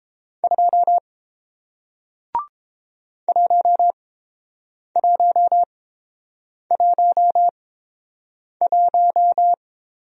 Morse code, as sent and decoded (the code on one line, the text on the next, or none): ..--- / .---- .---- .---- .----
2 1111